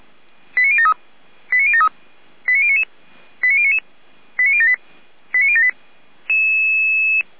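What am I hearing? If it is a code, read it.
Five-tone selective call (ZVEI), heard as quick runs of five tones, tones of about 70 ms. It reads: 89073 89073 89099 89099 89078 89078
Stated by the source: ZVEI five-tone selective call received on SDR.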